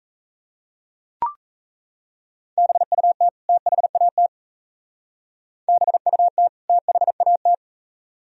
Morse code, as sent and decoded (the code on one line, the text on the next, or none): -... ..- - / - .... .- - / -... ..- - / - .... .- -
BUT THAT BUT THAT